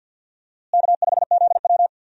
Morse code, heard as a keyed tone, about 50 wpm, 700 Hz. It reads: K5ZW
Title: USA Call Signs - 4 Characters - Repeated 3x - 50wpm